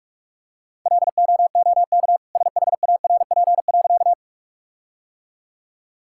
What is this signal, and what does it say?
Morse code, heard as a keyed tone, about 46 wpm, 700 Hz. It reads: LOOK SHARP.
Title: Sentences from Top 1600 Words 45wpm